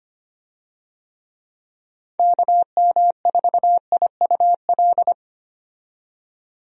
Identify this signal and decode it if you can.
Morse code, heard as a keyed tone, about 25 wpm, 700 Hz. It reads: KM4IUL
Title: USA Call Signs - 6 Characters - 25wpm